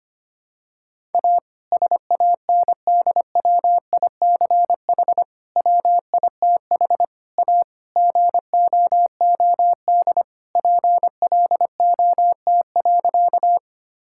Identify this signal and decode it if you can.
Morse code, heard as a keyed tone, about 25 wpm, 700 Hz. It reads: A SANDWICH WITH A GOOD PLOT.
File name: practice / 25wpm / All-Age Appropriate Jokes - 25wpm